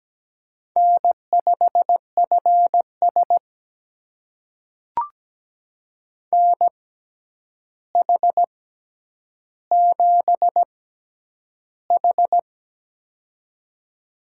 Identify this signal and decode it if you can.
Morse code, heard as a keyed tone, about 17 wpm, spaced as for 9 wpm, 700 Hz. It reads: N5FS NH7H
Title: USA Call Signs - 6x Character Spacing - 17wpm